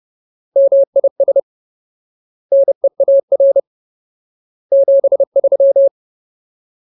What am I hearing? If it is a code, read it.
Morse code, heard as a keyed tone, about 30 wpm, 550 Hz. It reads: MIS NEAR 73
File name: Ultimate Word List / 30WPM Ultimate